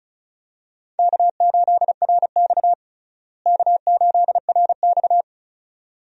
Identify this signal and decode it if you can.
Morse code, heard as a keyed tone, about 34 wpm, 700 Hz. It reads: K8RX K8RX
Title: USA Call Signs - 4 Characters - Repeated 2x - 35wpm